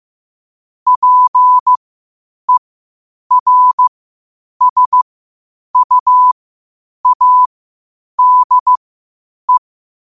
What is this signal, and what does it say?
Morse code, heard as a keyed tone, about 15 wpm, spaced as for 8 wpm, 1000 Hz. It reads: PERSUADE